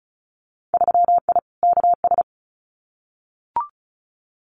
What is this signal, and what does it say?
Morse code, heard as a keyed tone, about 34 wpm, 700 Hz. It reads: 3I KS
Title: Winter Field Day Exchanges Encore 35wpm